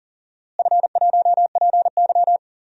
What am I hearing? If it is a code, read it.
Morse code, heard as a keyed tone, about 40 wpm, 700 Hz. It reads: F1PY
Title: DXCC-100 Call Signs - 4 Characters - Repeated 3x - 40wpm